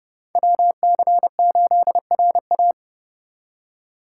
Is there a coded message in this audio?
Yes: Morse code from 0.3 to 2.7 s, heard as a keyed tone, about 39 wpm, 700 Hz. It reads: WC8RA